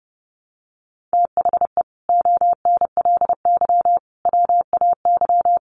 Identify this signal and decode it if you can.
Morse code, heard as a keyed tone, about 30 wpm, 700 Hz. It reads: THE ONLY WAY
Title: Sets of 5 Words Encore2 30wpm